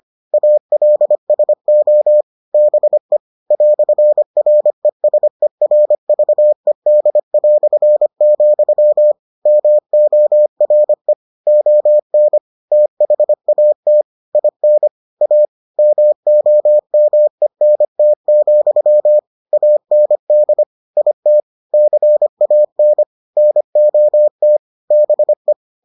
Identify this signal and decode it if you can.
Morse code, heard as a keyed tone, about 25 wpm, 600 Hz. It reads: ALSO BE "RESERVED", MORE ON THAT IN A MOMENT, AND IT CAN NOT BE